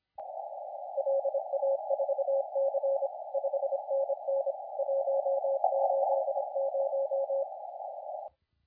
Morse code, heard as a keyed tone, about 25 wpm, 550 Hz, with noise. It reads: LA4C 5NN 180
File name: la4c / sac-cw-2017 / CONTEST AUDIO 1100Z